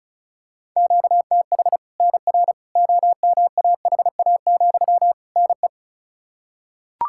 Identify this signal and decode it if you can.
Morse code, heard as a keyed tone, about 34 wpm, 700 Hz. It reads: QTH NR OMAHA, NE
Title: Practice QSO - Exchange 1 - Beginning - 35wpm